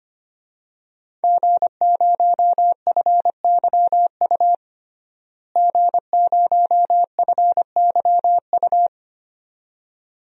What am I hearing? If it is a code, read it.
Morse code, heard as a keyed tone, about 25 wpm, 700 Hz. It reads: G0FYU G0FYU
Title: DXCC-100 Call Signs - 5 Characters - Repeated 2x - 25wpm